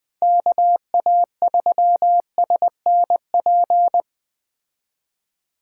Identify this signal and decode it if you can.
Morse code, heard as a keyed tone, about 20 wpm, 700 Hz. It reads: KA3SNP
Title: Speed Racing - USA 2x3 Call Signs 17wpm